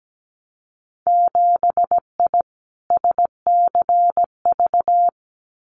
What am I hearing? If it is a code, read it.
Morse code, heard as a keyed tone, about 17 wpm, 700 Hz. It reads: 7I SCV
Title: Winter Field Day Exchanges Encore 17wpm